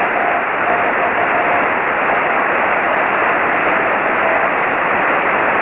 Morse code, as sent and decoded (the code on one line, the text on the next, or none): none